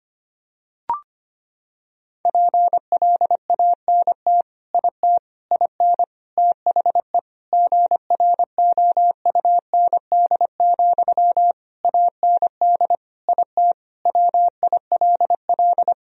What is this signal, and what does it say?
Morse code, heard as a keyed tone, about 25 wpm, 700 Hz. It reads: PLANT IT IN THE GROUND, AND IT WILL
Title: Sentences from Top 1000 Words - Review Part 2 - 25wpm